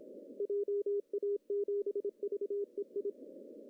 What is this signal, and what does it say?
Morse code, heard as a keyed tone, about 28 wpm, 400 Hz. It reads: JA7VEI
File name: CW / cqww-cw-k5tr